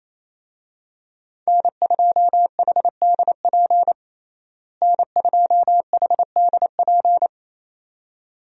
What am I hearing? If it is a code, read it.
Morse code, heard as a keyed tone, about 28 wpm, 700 Hz. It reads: N2HDP N2HDP
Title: USA Call Signs - 5 Characters - Repeated 2x - 28wpm